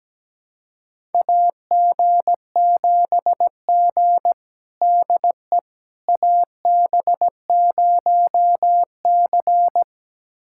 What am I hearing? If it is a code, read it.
Morse code, heard as a keyed tone, about 17 wpm, 700 Hz. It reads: AG7G DE AB0C
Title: USA Callsign DE Callsign - 4 Characters - 17wpm